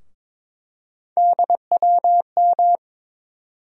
Morse code, heard as a keyed tone, about 22 wpm, 700 Hz. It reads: DWM
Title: USA Call Sign Postfixes - 3 Letters 22wpm